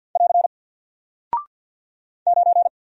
Morse code, heard as a keyed tone, about 49 wpm, 700 Hz. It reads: <AR> 9
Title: Single Character (Letter:Number:Punctuation:Prosign) 50wpm